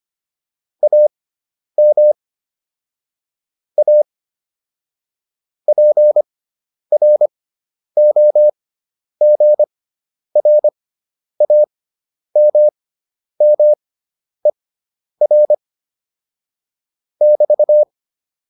Morse code, heard as a keyed tone, about 25 wpm, spaced as for 10 wpm, 600 Hz. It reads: AM A PROGRAMMER <BT>